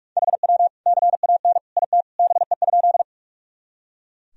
Morse code, heard as a keyed tone, about 42 wpm, 700 Hz. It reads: HW CAN IT BE?
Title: Sentences from Top 100 Words and Abbreviations 45wpm